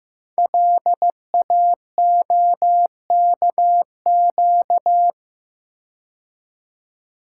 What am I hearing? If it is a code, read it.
Morse code, heard as a keyed tone, about 15 wpm, 700 Hz. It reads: LAOKQ
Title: Code Groups - 5 Letters - 15wpm